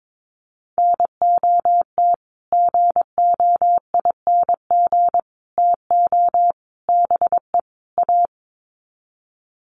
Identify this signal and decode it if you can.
Morse code, heard as a keyed tone, about 22 wpm, 700 Hz. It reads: NOT GOING TO BE A
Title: Sets of 5 Words Encore 22wpm